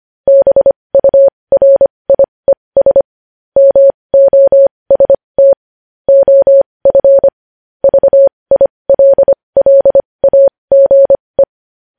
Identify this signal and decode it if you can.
Morse code, heard as a keyed tone, about 25 wpm, spaced as for 20 wpm, 550 Hz. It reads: BURIES MOST OF VILLAGE
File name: podcast / News headlines-1x-25x20